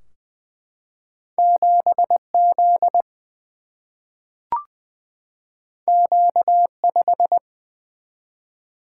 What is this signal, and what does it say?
Morse code, heard as a keyed tone, about 20 wpm, 700 Hz. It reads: 7 Z Q 5
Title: Single-Letter Single-Number 20wpm